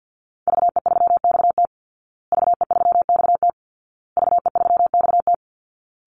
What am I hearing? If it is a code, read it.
Morse code, heard as a keyed tone, about 50 wpm, 700 Hz. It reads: VE3XT VE3XT VE3XT
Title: Call Signs Repeated 2x - 50wpm